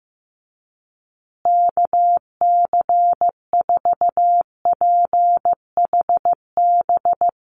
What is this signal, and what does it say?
Morse code, heard as a keyed tone, about 15 wpm, 700 Hz. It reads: KC4PHB
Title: USA Call Signs Repeated 2x 15wpm